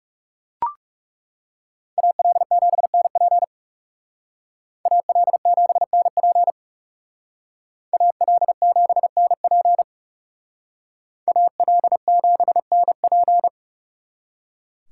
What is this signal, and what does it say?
Morse code, heard as a keyed tone, about 36 wpm, 700 Hz. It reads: AL7NP AL7NP AL7NP AL7NP
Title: Speed Racing - USA Call Signs 30wpm